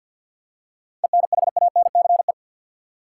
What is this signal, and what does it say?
Morse code, heard as a keyed tone, about 50 wpm, 700 Hz. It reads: ENHANCE
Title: Seven Letter Words - 50wpm